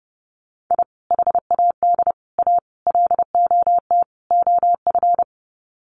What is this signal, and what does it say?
Morse code, heard as a keyed tone, about 30 wpm, 700 Hz. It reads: I HAD A LOT OF
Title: Sets of 5 Words Encore 30wpm